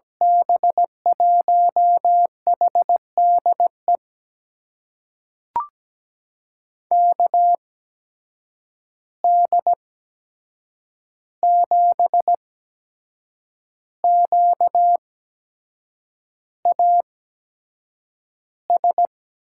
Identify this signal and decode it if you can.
Morse code, heard as a keyed tone, about 17 wpm, spaced as for 7 wpm, 700 Hz. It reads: B1HDE KD7QAS
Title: USA Call Signs - 6 Characters - 8x Character Spacing - 17wpm